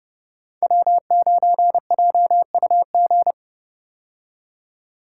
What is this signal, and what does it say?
Morse code, heard as a keyed tone, about 32 wpm, 700 Hz. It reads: W9JUG
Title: K1USN SST QSO Practice - 30wpm